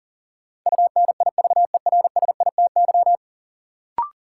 Morse code, heard as a keyed tone, about 37 wpm, 700 Hz. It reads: UNIVERSITY
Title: Sentences from Top 1300 Words - New Word Drill Alt - 40wpm